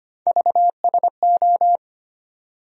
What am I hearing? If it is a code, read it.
Morse code, heard as a keyed tone, about 25 wpm, 700 Hz. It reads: VSO
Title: USA Call Sign Postfixes - 3 Letters 25wpm